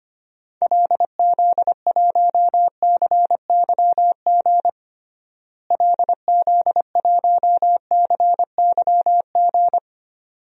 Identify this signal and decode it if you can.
Morse code, heard as a keyed tone, about 25 wpm, 700 Hz. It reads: LZ1CYG LZ1CYG
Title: DXCC-100 Call Signs - 6 Characters - Repeated 2x - 25wpm